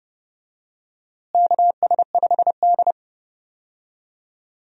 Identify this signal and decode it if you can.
Morse code, heard as a keyed tone, about 32 wpm, 700 Hz. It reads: KS5D
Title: USA Call Signs - 4 Characters - Repeated 3x - 30wpm